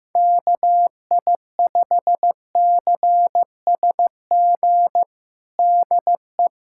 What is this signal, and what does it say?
Morse code, heard as a keyed tone, about 15 wpm, 700 Hz. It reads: KI5CSG DE
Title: USA Callsign DE Callsign - 6 Characters - 15wpm